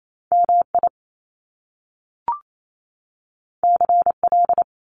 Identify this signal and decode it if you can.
Morse code, heard as a keyed tone, about 28 wpm, 700 Hz. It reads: MI CL
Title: Top Two Letter Combinations in Words 28wpm